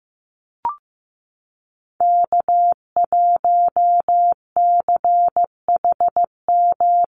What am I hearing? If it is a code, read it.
Morse code, heard as a keyed tone, about 14 wpm, 700 Hz. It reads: K1CHM
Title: Call Signs Repeated 3x - 15wpm